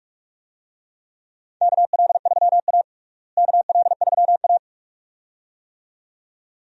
Morse code, heard as a keyed tone, about 41 wpm, 700 Hz. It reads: KL3A KL3A
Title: USA Call Signs - 4 Characters - Repeated 2x - 45wpm